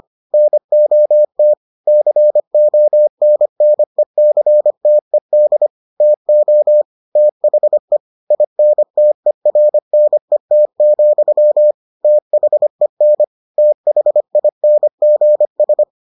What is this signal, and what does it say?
Morse code, heard as a keyed tone, about 25 wpm, 600 Hz. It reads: NOT CONNECTED TO THE INTERNET, THEN THINGS